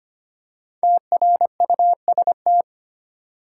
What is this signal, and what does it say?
Morse code, heard as a keyed tone, about 25 wpm, 700 Hz. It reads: TRUST
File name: 5 Letter Words - First 3 Letters Given - 25wpm